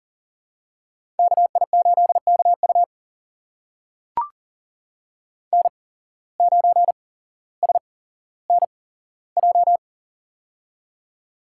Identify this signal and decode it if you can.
Morse code, heard as a keyed tone, about 40 wpm, spaced as for 17 wpm, 700 Hz. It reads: KI8KU N9SNJ
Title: USA Call Signs - 8x Character Spacing - 40wpm